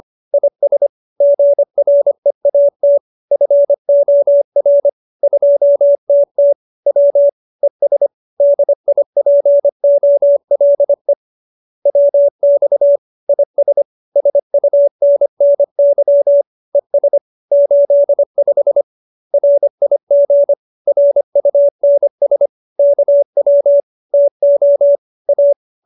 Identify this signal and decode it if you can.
Morse code, heard as a keyed tone, about 25 wpm, 570 Hz. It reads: IS GREAT FOR 2TT W ES DIPOLE WX IS SUNNY ES 85 RIG RUNS KW TO A